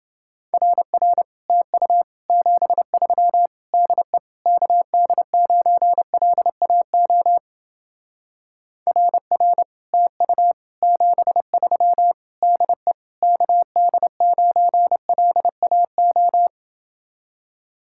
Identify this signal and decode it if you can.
Morse code, heard as a keyed tone, about 28 wpm, 700 Hz. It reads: RR TU 73 DE KD9LAO RR TU 73 DE KD9LAO